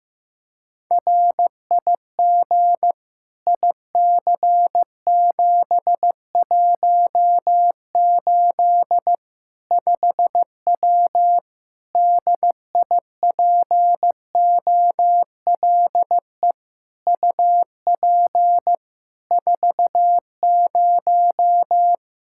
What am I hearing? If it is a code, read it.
Morse code, heard as a keyed tone, about 16 wpm, 700 Hz. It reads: RIG IC718 5W DIPOLE UP 40
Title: K1USN SST QSO Practice 15wpm